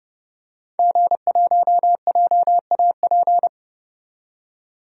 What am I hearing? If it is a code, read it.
Morse code, heard as a keyed tone, about 30 wpm, 700 Hz. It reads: G1JAP
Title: DXCC-100 Call Signs - 5 Characters - 30wpm